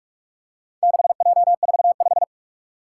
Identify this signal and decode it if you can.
Morse code, heard as a keyed tone, about 41 wpm, 700 Hz. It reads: 6J45
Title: Four Challenging Letter-Number Combinations 45wpm